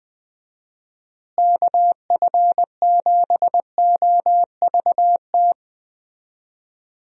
Morse code, heard as a keyed tone, about 20 wpm, 700 Hz. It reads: KF7OVT
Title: USA Call Signs - 6 Characters - Repeated 2x - 20wpm